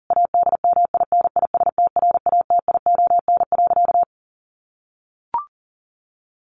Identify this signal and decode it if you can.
Morse code, heard as a keyed tone, about 40 wpm, 700 Hz. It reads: ADMINISTRATION.
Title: Sentences from Top 1300 Words 40wpm